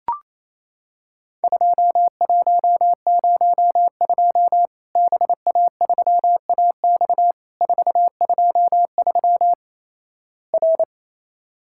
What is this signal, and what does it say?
Morse code, heard as a keyed tone, about 28 wpm, 700 Hz. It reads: 2102 BA3AX 423 R
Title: Worked All Europe DX Contest - QTC Exchange - 28wpm